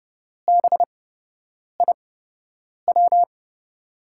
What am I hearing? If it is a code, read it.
Morse code, heard as a keyed tone, about 30 wpm, spaced as for 8 wpm, 700 Hz. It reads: BIW
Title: USA Call Signs - 6 Characters - 8x Character Spacing - 30wpm